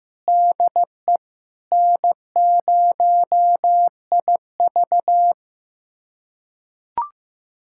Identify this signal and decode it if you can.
Morse code, heard as a keyed tone, about 15 wpm, 700 Hz. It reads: DE N0IV
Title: USA Callsign DE Callsign - 4 Characters - 15wpm